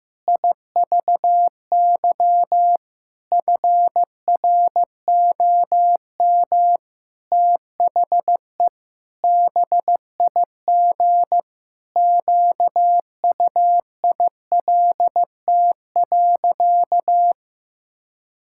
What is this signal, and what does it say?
Morse code, heard as a keyed tone, about 16 wpm, 700 Hz. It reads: IVY FROM THE BIG QUILT.
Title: Pangram Challenge v2 - 15wpm